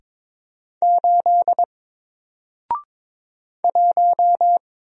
Morse code, heard as a keyed tone, about 22 wpm, 700 Hz. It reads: 8 1
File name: Single-Letter-Number-22wpm